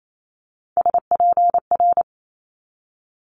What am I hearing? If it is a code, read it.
Morse code, heard as a keyed tone, about 28 wpm, 700 Hz. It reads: SPR
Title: Multi-letter Phonemes 28wpm